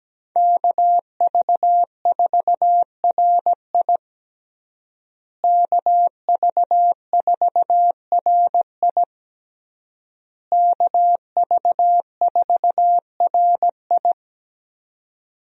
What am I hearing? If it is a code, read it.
Morse code, heard as a keyed tone, about 17 wpm, 700 Hz. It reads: KV4RI KV4RI KV4RI
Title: USA Call Signs - 5 Characters - Repeated 3x - 17wpm